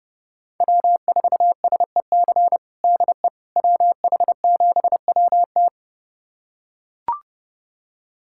Morse code, heard as a keyed tone, about 32 wpm, 700 Hz. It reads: W4SEC DE WH7WT
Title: USA Callsign DE Callsign - 5 Characters - 30wpm